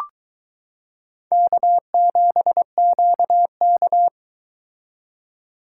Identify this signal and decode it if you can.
Morse code, heard as a keyed tone, about 22 wpm, 700 Hz. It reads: K7QK